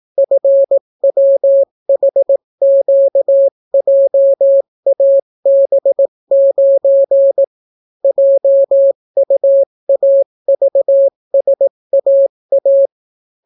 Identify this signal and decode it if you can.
Morse code, heard as a keyed tone, about 18 wpm, 550 Hz. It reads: FWHQJAB9 JUAVSAA